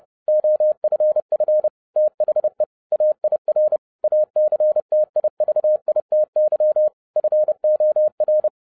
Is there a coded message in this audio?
Yes: Morse code from 0.3 to 8.5 s, heard as a keyed tone, about 30 wpm, 610 Hz. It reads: OFF THE AIR ACTIVITY FOR